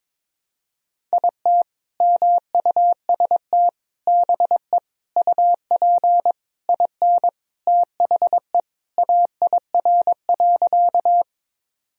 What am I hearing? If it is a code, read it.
Morse code, heard as a keyed tone, about 22 wpm, 700 Hz. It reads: IT MUST BE UP IN THE AIR.
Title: Sentences from Top 1000 Words - Review Part 1 - 22wpm